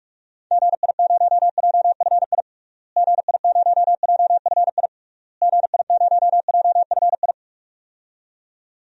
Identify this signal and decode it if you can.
Morse code, heard as a keyed tone, about 46 wpm, 700 Hz. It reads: GI0JFI GI0JFI GI0JFI
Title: DXCC-100 Call Signs - 6 Characters - Repeated 3x - 45wpm